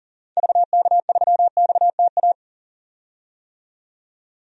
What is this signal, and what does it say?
Morse code, heard as a keyed tone, about 40 wpm, 700 Hz. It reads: VK3XTA
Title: DXCC-100 Call Signs - 6 Characters - 40wpm